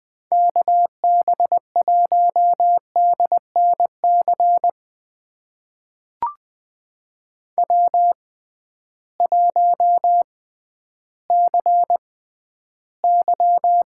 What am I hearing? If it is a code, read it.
Morse code, heard as a keyed tone, about 21 wpm, spaced as for 12 wpm, 700 Hz. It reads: KB1DNC W1CY
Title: USA Call Signs - 6x Character Spacing - 20wpm